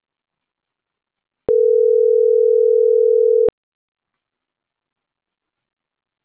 Ringback tone (North America), 440 + 480 Hz.